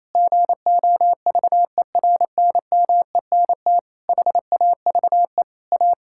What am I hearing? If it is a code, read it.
Morse code, heard as a keyed tone, about 28 wpm, 700 Hz. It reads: GOVERNMENT HAVE A
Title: Sentences from Top 1600 Words 28wpm